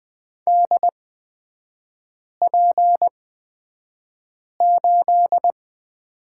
Morse code, heard as a keyed tone, about 20 wpm, spaced as for 5 wpm, 690 Hz. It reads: DP8
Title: 229 Lesson 37e - Letter x - Three Characters 20wpm